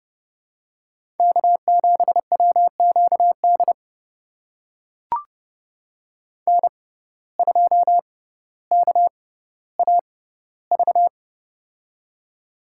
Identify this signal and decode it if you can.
Morse code, heard as a keyed tone, about 30 wpm, spaced as for 15 wpm, 700 Hz. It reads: K7WQD N2KAV
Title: USA Call Signs - 5 Characters - 6x Character Spacing - 30wpm